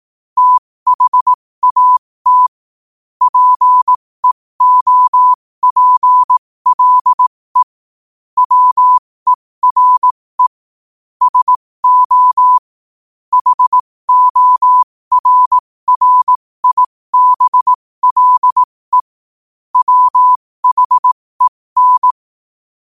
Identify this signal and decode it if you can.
Morse code, heard as a keyed tone, about 18 wpm, spaced as for 15 wpm, 1000 Hz. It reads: THAT PEOPLE WERE SO HORRIBLE WHEN
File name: qotdfiles / 15wpm